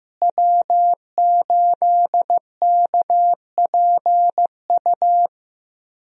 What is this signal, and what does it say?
Morse code, heard as a keyed tone, about 15 wpm, 700 Hz. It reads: W8KPU
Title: USA Call Signs - 5 Characters - Repeated 2x - 15wpm